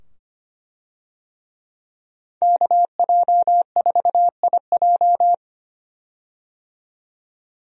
Morse code, heard as a keyed tone, about 25 wpm, 700 Hz. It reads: KJ4IJ
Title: USA Call Signs - 5 Characters - 25wpm